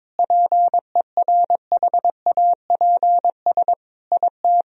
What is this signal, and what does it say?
Morse code, heard as a keyed tone, about 22 wpm, 700 Hz. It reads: PERHAPS IT
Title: Sentences from Top 500 Words and Abbreviations 22wpm